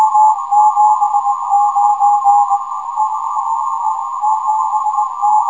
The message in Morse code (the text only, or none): none